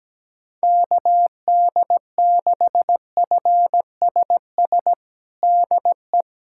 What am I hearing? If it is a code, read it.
Morse code, heard as a keyed tone, about 17 wpm, 700 Hz. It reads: KD6FSS DE
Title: USA Callsign DE Callsign 17wpm